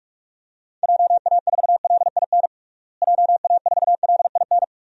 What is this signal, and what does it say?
Morse code, heard as a keyed tone, about 47 wpm, 700 Hz. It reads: JA4LIN JA4LIN
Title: DXCC-100 Call Signs - 6 Characters - Repeated 2x - 45wpm